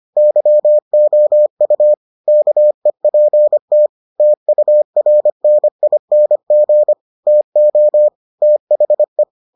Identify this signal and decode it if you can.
Morse code, heard as a keyed tone, about 25 wpm, 600 Hz. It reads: YOU KEPT TURNING TO THE